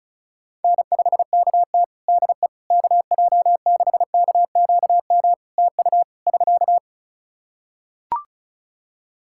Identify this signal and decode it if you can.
Morse code, heard as a keyed tone, about 36 wpm, 700 Hz. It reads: N5KT DE KJ6KQM TU <SK>
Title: Practice QSO - Exchange 4 - Ending - 35wpm